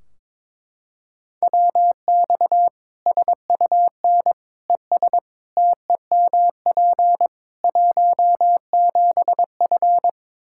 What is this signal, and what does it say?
Morse code, heard as a keyed tone, about 22 wpm, 700 Hz. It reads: WX SUN ES TEMP 17F
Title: Practice QSO - Exchange 2 - Info - 22wpm